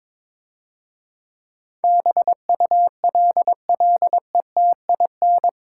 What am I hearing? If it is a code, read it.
Morse code, heard as a keyed tone, about 23 wpm, 700 Hz. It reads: BULLETIN
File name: Easy Words - Speed-Racing with Hints - 22wpm